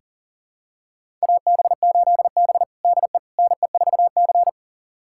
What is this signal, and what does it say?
Morse code, heard as a keyed tone, about 40 wpm, 700 Hz. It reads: AB8B DE NE4C
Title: USA Callsign DE Callsign - 4 Characters - 40wpm